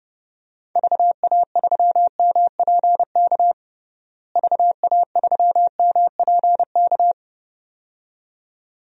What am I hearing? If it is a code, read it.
Morse code, heard as a keyed tone, about 30 wpm, 700 Hz. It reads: VA3MPK VA3MPK
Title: DXCC-100 Call Signs - 6 Characters - Repeated 2x - 30wpm